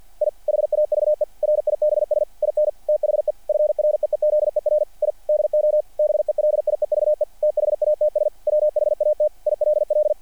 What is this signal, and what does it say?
Morse code, heard as a keyed tone, about 48 wpm, 600 Hz. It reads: I HAVE RIBS IN THE FREEZER I DO BELIEVE THATS WHAT ILL
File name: archived / CRSnet